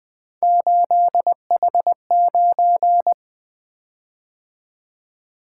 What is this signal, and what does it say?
Morse code, heard as a keyed tone, about 20 wpm, 700 Hz. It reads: 8H9
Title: Three Letter-Number Combinations - 20wpm